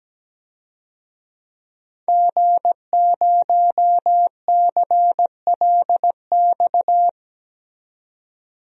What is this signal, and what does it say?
Morse code, heard as a keyed tone, about 17 wpm, 700 Hz. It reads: G0CLX
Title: DXCC-100 Call Signs - 5 Characters - 17wpm